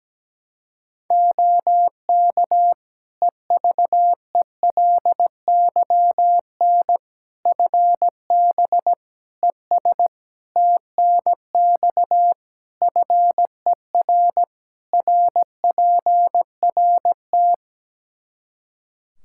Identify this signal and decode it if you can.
Morse code, heard as a keyed tone, about 17 wpm, 700 Hz. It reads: OK EVELYN FB ES TNX FER RPRT
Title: Practice QSO - Exchange 2 - Info - 17wpm